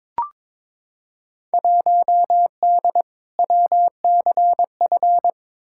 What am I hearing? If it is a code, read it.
Morse code, heard as a keyed tone, about 22 wpm, 700 Hz. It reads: 1D WCF
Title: Field Day Exchanges - 22wpm